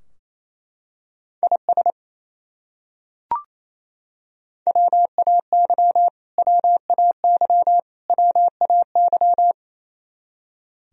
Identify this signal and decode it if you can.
Morse code, heard as a keyed tone, about 27 wpm, 700 Hz. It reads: IS WAY WAY WAY
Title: Top 100 Words Repeated 28wpm